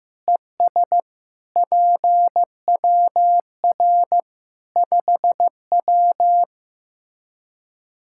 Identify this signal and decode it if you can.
Morse code, heard as a keyed tone, about 15 wpm, 700 Hz. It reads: ES PWR 5W
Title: Practice QSO - Exchange 2 - Info - 15wpm